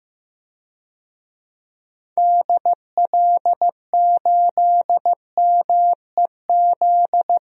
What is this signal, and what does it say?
Morse code, heard as a keyed tone, about 15 wpm, 700 Hz. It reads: DL8MEZ
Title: DXCC-100 Call Signs - 6 Characters - Repeated 2x - 15wpm